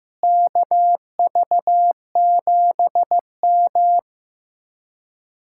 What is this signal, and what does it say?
Morse code, heard as a keyed tone, about 15 wpm, 700 Hz. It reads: KV7M
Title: Speed-Racing Common QSO Phrases Encore - 15wpm